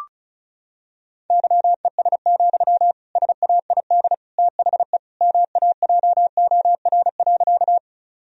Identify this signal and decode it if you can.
Morse code, heard as a keyed tone, about 34 wpm, 700 Hz. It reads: YES, SAID THE MAJOR.
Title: Sentences from Top 1000 Words - Review Part 4 - 35wpm